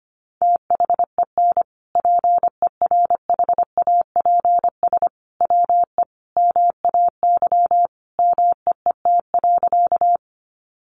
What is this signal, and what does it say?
Morse code, heard as a keyed tone, about 25 wpm, 700 Hz. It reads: THEN PERHAPS WE MAY MEET.
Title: Sentences from Top 1100 Words 25wpm